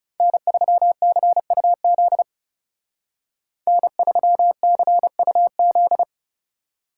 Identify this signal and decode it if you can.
Morse code, heard as a keyed tone, about 36 wpm, 700 Hz. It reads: N3CUZ N3CUZ
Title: Speed Racing - USA Call Signs 30wpm